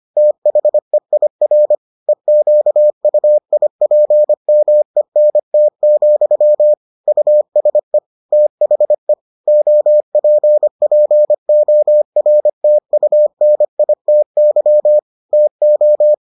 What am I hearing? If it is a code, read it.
Morse code, heard as a keyed tone, about 25 wpm, 600 Hz. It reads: THEIR EQUIPMENT, USE THE OPPORTUNITY TO